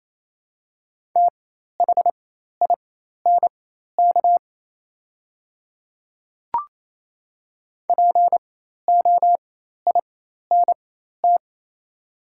Morse code, heard as a keyed tone, about 28 wpm, spaced as for 13 wpm, 700 Hz. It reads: THINK POINT
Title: 4x Character Spacing - Top 200 Words - 28wpm